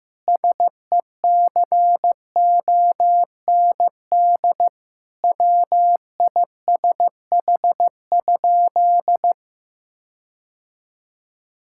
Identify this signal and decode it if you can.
Morse code, heard as a keyed tone, about 15 wpm, 700 Hz. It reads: SECOND WISH?
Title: Sentences from Top 1600 Words 15wpm